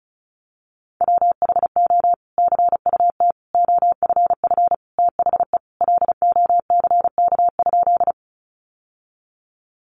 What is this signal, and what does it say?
Morse code, heard as a keyed tone, about 34 wpm, 700 Hz. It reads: WHO CUT OFF THE LOCK?
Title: Sentences from Top 1300 Words 35wpm